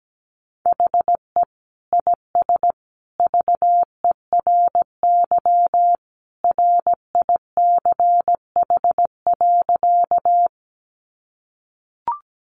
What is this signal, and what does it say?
Morse code, heard as a keyed tone, about 17 wpm, 700 Hz. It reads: HE IS VERY RICH.
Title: Sentences from Top 1100 Words 17wpm